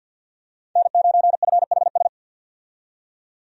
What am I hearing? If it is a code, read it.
Morse code, heard as a keyed tone, about 50 wpm, 700 Hz. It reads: N9FHS
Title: USA Call Signs - 5 Characters - 50wpm